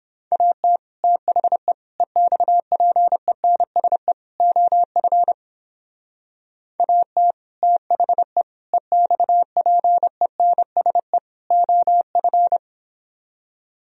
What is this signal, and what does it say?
Morse code, heard as a keyed tone, about 30 wpm, 700 Hz. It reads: AT THE EXPENSE OF AT THE EXPENSE OF